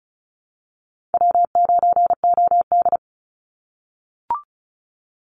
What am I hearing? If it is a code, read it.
Morse code, heard as a keyed tone, about 34 wpm, 700 Hz. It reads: W9OD
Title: Call Signs - 35wpm